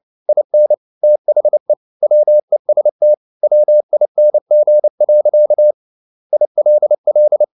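Morse code, heard as a keyed tone, about 29 wpm, 600 Hz. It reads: IN THE WEST WING. ILL
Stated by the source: Morse code at 29 wpm, sending Ken Follett's novel Never in English, from CW med Gnister.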